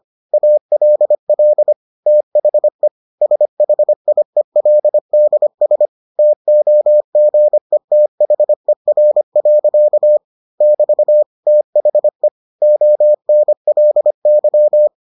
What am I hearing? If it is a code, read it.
Morse code, heard as a keyed tone, about 25 wpm, 600 Hz. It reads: ALL THE SHIELDS TOGETHER. <BT> THE ONLY